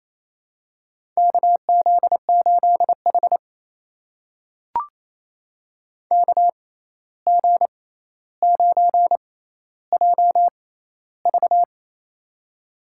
Morse code, heard as a keyed tone, about 28 wpm, spaced as for 13 wpm, 700 Hz. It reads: KZ8H KG9JV
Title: USA Call Signs - 6x Character Spacing - 28wpm